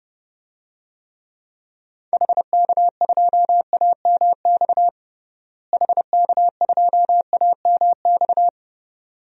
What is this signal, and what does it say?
Morse code, heard as a keyed tone, about 31 wpm, 700 Hz. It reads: HK2AMX HK2AMX
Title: DXCC-100 Call Signs - 6 Characters - Repeated 2x - 30wpm